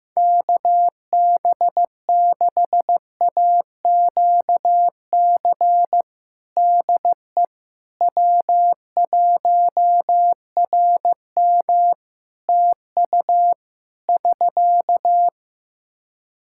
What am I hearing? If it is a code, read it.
Morse code, heard as a keyed tone, about 15 wpm, 700 Hz. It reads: KB6AQC DE W1RM TU <SK>
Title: Practice QSO - Exchange 4 - Ending - 15wpm